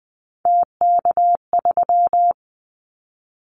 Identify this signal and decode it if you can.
Morse code, heard as a keyed tone, about 20 wpm, 700 Hz. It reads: TK3
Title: Callsign Prefixes for Top 100 DXCC - 20wpm